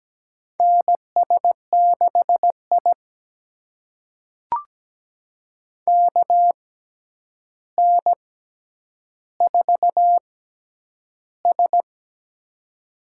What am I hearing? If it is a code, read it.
Morse code, heard as a keyed tone, about 17 wpm, spaced as for 9 wpm, 700 Hz. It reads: NS6I KN4S
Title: USA Call Signs - 4 Characters - 6x Character Spacing - 17wpm